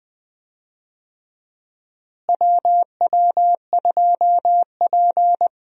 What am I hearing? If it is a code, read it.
Morse code, heard as a keyed tone, about 20 wpm, 700 Hz. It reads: WW2P
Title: USA Call Signs - 4 Characters - Repeated 2x - 20wpm